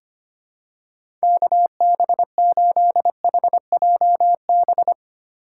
Morse code, heard as a keyed tone, about 25 wpm, 700 Hz. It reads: KB8HJB
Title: USA Call Signs - 6 Characters - 3x Character Spacing - 25wpm